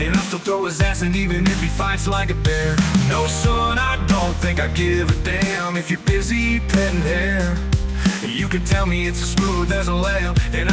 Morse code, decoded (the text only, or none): none